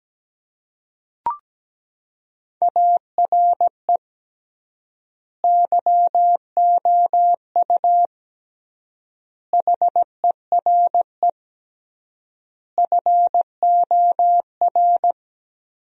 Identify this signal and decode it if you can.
Morse code, heard as a keyed tone, about 17 wpm, 700 Hz. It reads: ARE YOU HERE FOR
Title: 3x Word-Spacing - Sentences from Top 200 Words - 17wpm